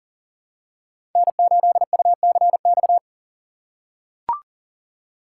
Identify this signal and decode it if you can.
Morse code, heard as a keyed tone, about 40 wpm, 700 Hz. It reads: N8UCX
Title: USA Call Signs - 4x Character Spacing - 40wpm